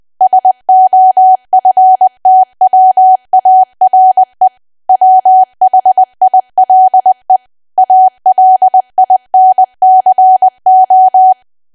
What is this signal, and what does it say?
Morse code, heard as a keyed tone, about 20 wpm, 750 Hz. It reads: SOFTWARE WHILE ALINCO
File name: Archive / 20 WPM / 20WPM